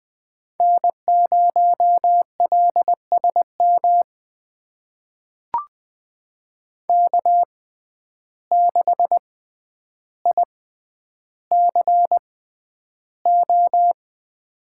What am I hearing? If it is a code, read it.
Morse code, heard as a keyed tone, about 20 wpm, spaced as for 10 wpm, 700 Hz. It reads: N0LSM K6ICO
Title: USA Call Signs - 6x Character Spacing - 20wpm